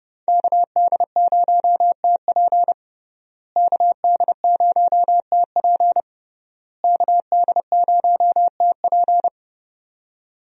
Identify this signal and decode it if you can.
Morse code, heard as a keyed tone, about 30 wpm, 700 Hz. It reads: KD0TP KD0TP KD0TP
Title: USA Call Signs - 5 Characters - Repeated 3x - 30wpm